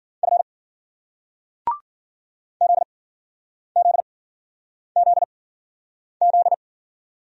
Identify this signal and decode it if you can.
Morse code, heard as a keyed tone, about 56 wpm, spaced as for 9 wpm, 700 Hz. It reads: F ZZZZ